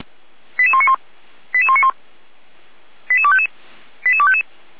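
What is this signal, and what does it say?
Five-tone selective call (ZVEI), heard as quick runs of five tones, tones of about 70 ms. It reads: 88191 88191 88255 88255
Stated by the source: ZVEI five-tone selective call 88571 received on an SDR.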